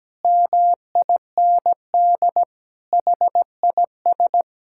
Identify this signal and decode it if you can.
Morse code, heard as a keyed tone, about 17 wpm, 700 Hz. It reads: MIND HIS
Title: Sentences from Top 1000 Words - Review Part 6 - 17wpm